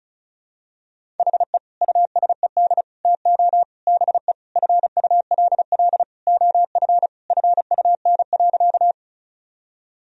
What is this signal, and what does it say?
Morse code, heard as a keyed tone, about 36 wpm, 700 Hz. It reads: HE USED TO BE FULL OF FUN.
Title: Sentences from Top 1000 Words - Review Part 4 - 35wpm